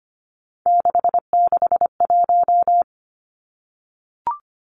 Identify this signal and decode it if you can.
Morse code, heard as a keyed tone, about 25 wpm, 700 Hz. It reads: 661
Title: Three Challenging Letter-Number Combinations 25wpm